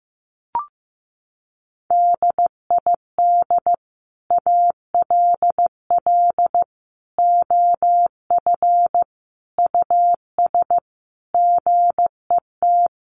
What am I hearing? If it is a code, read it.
Morse code, heard as a keyed tone, about 14 wpm, 700 Hz. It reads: DID ALL OF US GET
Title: A Sound Of Thunder 15wpm